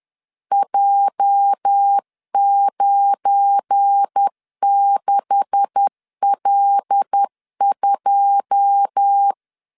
Morse code, heard as a keyed tone, about 10 wpm, 800 Hz. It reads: J96L2
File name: CW / DET / 1 9 1500C S B 10wpm 4439